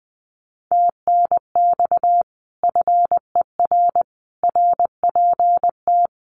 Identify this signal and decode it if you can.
Morse code, heard as a keyed tone, about 20 wpm, 700 Hz. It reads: TNX FER RPT
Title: Common QSO Phrases Encore 20wpm